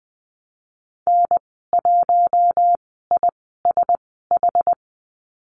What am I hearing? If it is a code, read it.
Morse code, heard as a keyed tone, about 20 wpm, spaced as for 14 wpm, 700 Hz. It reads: N1ISH
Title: 069 Lesson 12g - Letter d - Call Signs 20wpm